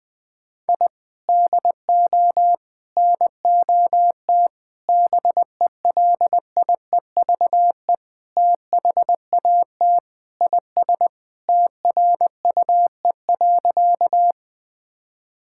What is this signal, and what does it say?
Morse code, heard as a keyed tone, about 21 wpm, 700 Hz. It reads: I DO NOT BELIEVE THAT IS TRUE.
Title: Sentences with 4-Word Phrases - 20wpm